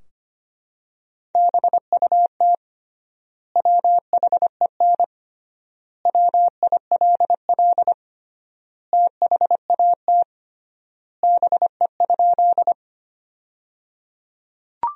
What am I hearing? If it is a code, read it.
Morse code, heard as a keyed tone, about 25 wpm, 700 Hz. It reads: BUT WHEN WILL THAT BE?
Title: 3x Word-Spacing - Sentences from Top 100 Words - 25wpm